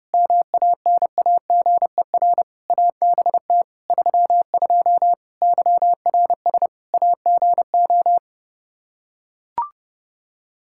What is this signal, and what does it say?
Morse code, heard as a keyed tone, about 32 wpm, 700 Hz. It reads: MANAGER ABT 32 YRS AGO
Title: Practice QSO - Exchange 3 - More Info - 30wpm